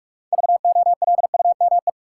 Morse code, heard as a keyed tone, about 46 wpm, 700 Hz. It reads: VOLUME
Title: Six Letter Words - 45wpm